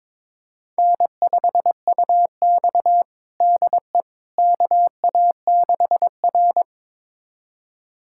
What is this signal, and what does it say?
Morse code, heard as a keyed tone, about 21 wpm, 700 Hz. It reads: N5UX DE KA6R
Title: USA Callsign DE Callsign - 4 Characters - 22wpm